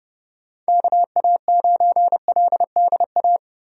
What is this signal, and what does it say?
Morse code, heard as a keyed tone, about 31 wpm, 700 Hz. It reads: KA9LDA
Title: USA Call Signs - 6 Characters - 30wpm